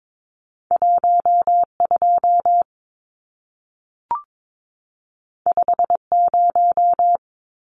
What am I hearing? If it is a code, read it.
Morse code, heard as a keyed tone, about 22 wpm, 700 Hz. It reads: 12 50
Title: Numbers - 2-Digit 22wpm